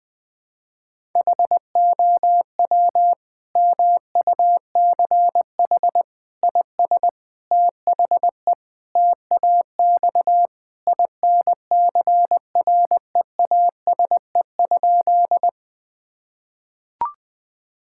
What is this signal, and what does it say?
Morse code, heard as a keyed tone, about 21 wpm, 700 Hz. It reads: HOW MUCH IS THE TAX INCREASE?
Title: Sentences from Top 1000 Words - Review Part 5 - 20wpm